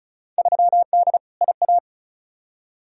Morse code, heard as a keyed tone, about 36 wpm, 690 Hz. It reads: 3D IA
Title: Field Day Exchanges Encore - 35wpm